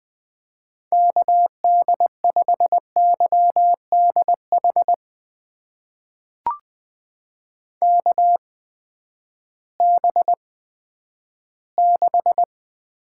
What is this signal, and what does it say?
Morse code, heard as a keyed tone, about 20 wpm, spaced as for 11 wpm, 700 Hz. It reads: KD5YDH KB6
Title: USA Call Signs - 6 Characters - 8x Character Spacing - 20wpm